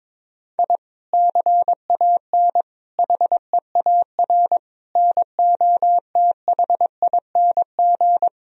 Morse code, heard as a keyed tone, about 22 wpm, 700 Hz. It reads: I CAN HEAR NOTHING
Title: Sentences from Top 1000 Words - Review Part 4 - 22wpm